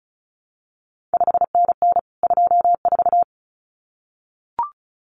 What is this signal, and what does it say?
Morse code, heard as a keyed tone, about 36 wpm, 700 Hz. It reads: 5NN 24
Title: CQ WW DX Contest Exchange Encore - 35wpm